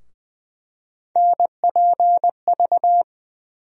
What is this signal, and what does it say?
Morse code, heard as a keyed tone, about 20 wpm, 700 Hz. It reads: NPV
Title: USA Call Sign Postfixes - 3 Letters 20wpm